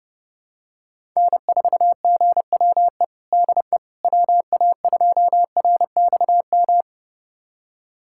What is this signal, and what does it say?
Morse code, heard as a keyed tone, about 30 wpm, 700 Hz. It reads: N4GWE DE WA2RXM
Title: USA Callsign DE Callsign 30wpm